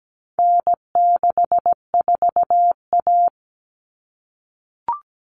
Morse code, heard as a keyed tone, about 17 wpm, 700 Hz. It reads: N64A
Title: Common Cut Numbers - 4-Digit 17wpm